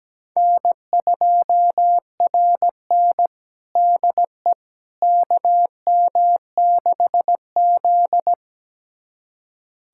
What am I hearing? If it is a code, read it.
Morse code, heard as a keyed tone, about 17 wpm, 700 Hz. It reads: N2RN DE KM6Z
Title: USA Callsign DE Callsign - 4 Characters - 17wpm